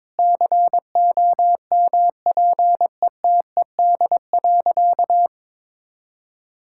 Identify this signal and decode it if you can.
Morse code, heard as a keyed tone, about 22 wpm, 700 Hz. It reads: COMPETED.
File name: practice / 22wpm / Sentences with 3-Word Phrases - 22wpm